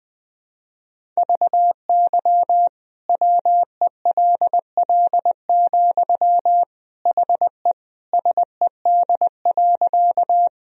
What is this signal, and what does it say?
Morse code, heard as a keyed tone, about 20 wpm, 700 Hz. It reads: VY WELL, HE SED.
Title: Sentences from Top 200 Words and Abbreviations 20wpm